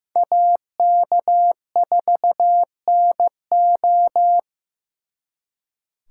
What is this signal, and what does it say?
Morse code, heard as a keyed tone, about 18 wpm, 700 Hz. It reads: AK4NO
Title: Speed Racing - USA 2x2 Call Signs 15wpm